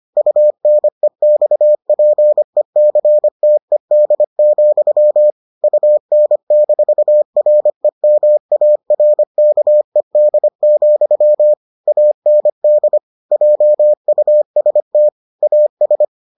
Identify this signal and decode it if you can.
Morse code, heard as a keyed tone, about 25 wpm, 600 Hz. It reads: UNEXPECTED, UN-REMARKED, AND JUST AS